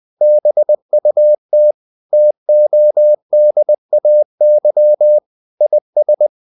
Morse code, heard as a keyed tone, about 20 wpm, 600 Hz. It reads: BUT TODAY IS
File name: the old man and the sea / the old man and the sea part 1 20wpm